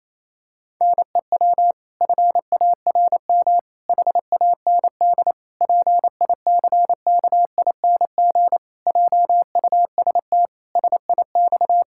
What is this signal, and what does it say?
Morse code, heard as a keyed tone, about 28 wpm, 700 Hz. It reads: NEW FARM HAND PICKING JUST SIX
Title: Pangram Challenge v2 - 28wpm